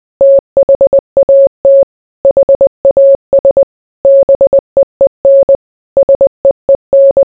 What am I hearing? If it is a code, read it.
Morse code, heard as a keyed tone, about 20 wpm, 550 Hz. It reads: THAT HAS BEEN SEEN